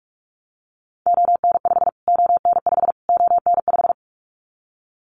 Morse code, heard as a keyed tone, about 44 wpm, 700 Hz. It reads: ON5 ON5 ON5
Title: Callsign Prefixes for Top 25 DXCC - Repeated 3x - 45wpm